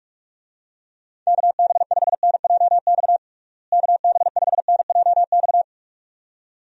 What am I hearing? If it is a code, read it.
Morse code, heard as a keyed tone, about 46 wpm, 700 Hz. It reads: KB5NJX KB5NJX
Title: USA Call Signs - 6 Characters - Repeated 2x - 45wpm